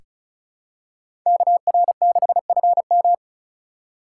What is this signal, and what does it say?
Morse code, heard as a keyed tone, about 36 wpm, 700 Hz. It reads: KR6FM
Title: USA Call Signs - 5 Characters - 4x Character Spacing - 35wpm